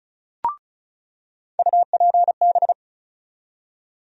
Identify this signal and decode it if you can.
Morse code, heard as a keyed tone, about 34 wpm, 700 Hz. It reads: UPB
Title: USA Call Sign Postfixes - 3 Letters 35wpm